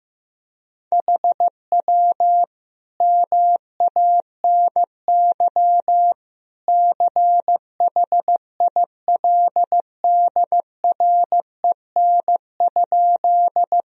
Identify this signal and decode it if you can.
Morse code, heard as a keyed tone, about 15 wpm, 700 Hz. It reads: HW MANY CHILDREN?
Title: Sentences from Top 300 Words and Abbreviations 15wpm